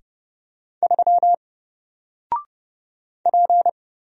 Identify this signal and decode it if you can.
Morse code, heard as a keyed tone, about 30 wpm, 700 Hz. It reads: 3 P
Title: Single Letter-Number 30wpm